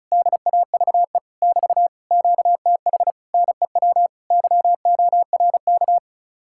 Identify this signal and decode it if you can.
Morse code, heard as a keyed tone, about 43 wpm, 700 Hz. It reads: DAVE <BT> QTH NEW YORK
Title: Speed-Racing Common QSO Phrases - 35wpm